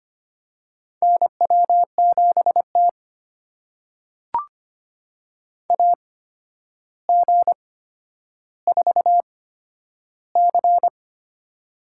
Morse code, heard as a keyed tone, about 25 wpm, spaced as for 11 wpm, 700 Hz. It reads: NW7T AG4C
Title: USA Call Signs - 4 Characters - 8x Character Spacing - 25wpm